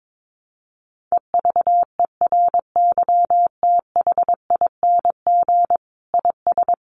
Morse code, heard as a keyed tone, about 22 wpm, 700 Hz. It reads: EVERYTHING IS
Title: Sentences from Top 700 Words 22wpm